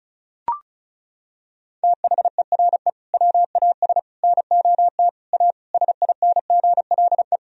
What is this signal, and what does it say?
Morse code, heard as a keyed tone, about 34 wpm, 700 Hz. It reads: THERE WAS NOT A SINGLE
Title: Sentences from Top 1000 Words - Review Part 4 - 35wpm